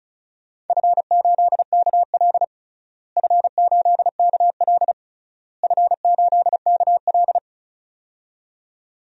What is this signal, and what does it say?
Morse code, heard as a keyed tone, about 34 wpm, 700 Hz. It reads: F8KL F8KL F8KL
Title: DXCC-100 Call Signs - 4 Characters - Repeated 3x - 35wpm